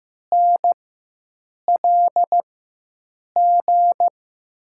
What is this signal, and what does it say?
Morse code, heard as a keyed tone, about 15 wpm, spaced as for 7 wpm, 700 Hz. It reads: NLG
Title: USA Call Signs - 4x Character Spacing - 15wpm